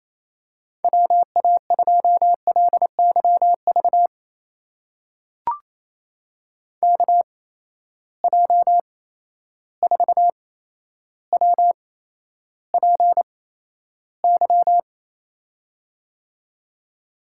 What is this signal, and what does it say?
Morse code, heard as a keyed tone, about 28 wpm, spaced as for 12 wpm, 700 Hz. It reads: WA2LYV KJ4WPY
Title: USA Call Signs - 8x Character Spacing - 28wpm